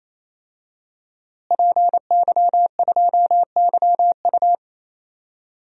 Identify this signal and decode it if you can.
Morse code, heard as a keyed tone, about 28 wpm, 700 Hz. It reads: PY2YU
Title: DXCC-100 Call Signs - 5 Characters - Repeated 3x - 28wpm